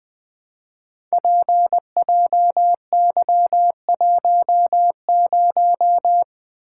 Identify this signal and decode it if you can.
Morse code, heard as a keyed tone, about 20 wpm, 700 Hz. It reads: PJY10
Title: Code Groups - 5 Letter:Numbers - 20wpm